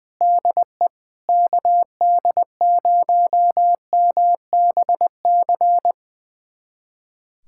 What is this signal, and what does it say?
Morse code, heard as a keyed tone, about 20 wpm, 700 Hz. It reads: DE KD0MBC
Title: USA Callsign DE Callsign - 6 Characters - 20wpm